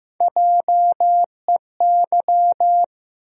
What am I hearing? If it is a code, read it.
Morse code, heard as a keyed tone, about 15 wpm, 700 Hz. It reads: JEY
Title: USA Call Sign Postfixes - 3 Letters 15wpm